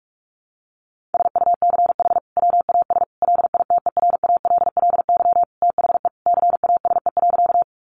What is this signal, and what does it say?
Morse code, heard as a keyed tone, about 44 wpm, 700 Hz. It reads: SUCH WAS LITERALLY THE CASE.
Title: Sentences from Top 1400 Words 45wpm